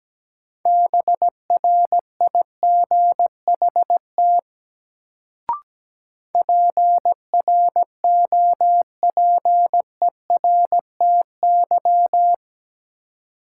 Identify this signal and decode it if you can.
Morse code, heard as a keyed tone, about 17 wpm, 700 Hz. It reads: BRIGHT PROPERTY
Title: Sentences from Top 1000 Words - New Word Drill - 17wpm